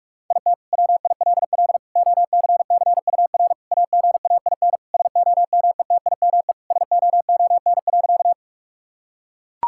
Morse code, heard as a keyed tone, about 41 wpm, 700 Hz. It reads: IT WILL OCCUR AGAIN SOMETIME SOON.
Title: Sentences from Top 1000 Words - Review Part 6 - 45wpm